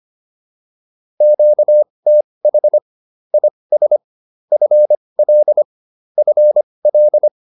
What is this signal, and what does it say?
Morse code, heard as a keyed tone, about 25 wpm, spaced as for 20 wpm, 600 Hz. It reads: QTH IS FL FL